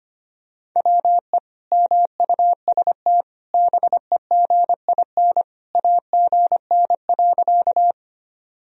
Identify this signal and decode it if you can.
Morse code, heard as a keyed tone, about 25 wpm, 700 Hz. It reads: WE MUST BEGIN AGN.
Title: Sentences from Top 300 Words and Abbreviations 25wpm